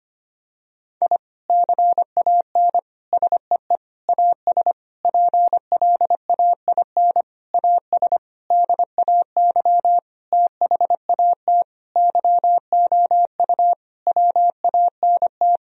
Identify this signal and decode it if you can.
Morse code, heard as a keyed tone, about 25 wpm, 700 Hz. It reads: I CAN SEE AS PLAIN AS DAY THAT YOU WANT
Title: Sentences from Top 1000 Words - Review Part 2 - 25wpm